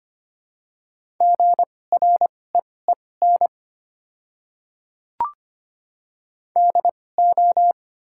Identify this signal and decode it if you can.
Morse code, heard as a keyed tone, about 25 wpm, spaced as for 18 wpm, 700 Hz. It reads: GREEN DO
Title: 2x Character Spacing - Top 500 Words - 25wpm